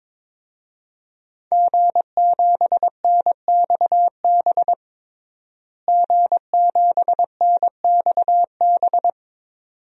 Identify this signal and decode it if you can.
Morse code, heard as a keyed tone, about 22 wpm, 700 Hz. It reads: G7NXB G7NXB
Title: DXCC-100 Call Signs - 5 Characters - Repeated 2x - 22wpm